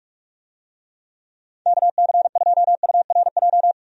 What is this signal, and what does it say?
Morse code, heard as a keyed tone, about 42 wpm, 700 Hz. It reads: KC2URJ
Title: USA Call Signs - 6 Characters - 45wpm